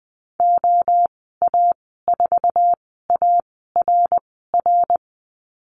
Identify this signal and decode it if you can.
Morse code, heard as a keyed tone, about 20 wpm, spaced as for 14 wpm, 700 Hz. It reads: OA4ARR
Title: 082 Lesson 14g - Number 2 - Call Signs 20wpm